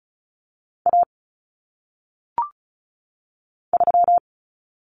Morse code, heard as a keyed tone, about 36 wpm, 700 Hz. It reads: A 3
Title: Common Cut Numbers - 1-Digit 35wpm